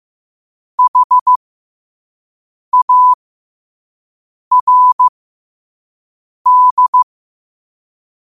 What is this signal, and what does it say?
Morse code, heard as a keyed tone, about 15 wpm, spaced as for 5 wpm, 1000 Hz. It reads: HARD
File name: qotdfiles / 5wpm